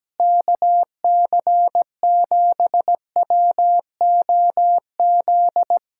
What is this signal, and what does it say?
Morse code, heard as a keyed tone, about 17 wpm, 700 Hz. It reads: KC7WOZ
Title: Speed-Racing Common QSO Phrases Encore - 17wpm